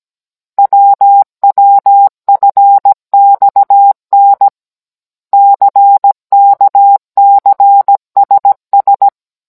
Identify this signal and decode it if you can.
Morse code, heard as a keyed tone, about 17 wpm, 800 Hz. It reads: WWFXN CKCSS
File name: Cours-NR18-16mm